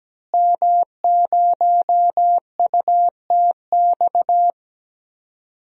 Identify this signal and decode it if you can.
Morse code, heard as a keyed tone, about 17 wpm, 700 Hz. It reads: M0UTX
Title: DXCC-100 Call Signs - 5 Characters - Repeated 2x - 17wpm